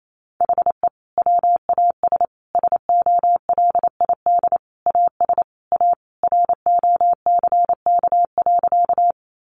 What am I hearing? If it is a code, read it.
Morse code, heard as a keyed tone, about 28 wpm, 700 Hz. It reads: HE WAS SOLID AS A ROCK.
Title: Sentences from Top 1500 Words 28wpm